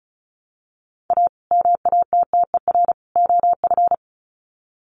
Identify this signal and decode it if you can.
Morse code, heard as a keyed tone, about 34 wpm, 700 Hz. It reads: A MATTER OF
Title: Sets of 3 Words 35wpm